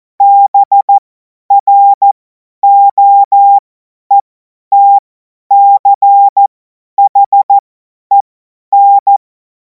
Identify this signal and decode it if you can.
Morse code, heard as a keyed tone, about 14 wpm, spaced as for 10 wpm, 800 Hz. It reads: BROETCHEN
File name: Morsekurs / Witze / Witz 10. März 2026 50 F70